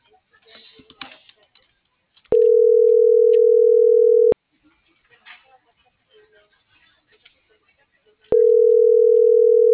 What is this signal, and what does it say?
Ringback tone (North America), 440 + 480 Hz.